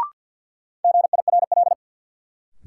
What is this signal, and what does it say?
Morse code, heard as a keyed tone, about 52 wpm, 700 Hz. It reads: GIRL
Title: Sentences from Top 400 Words - New Word Drill - 50wpm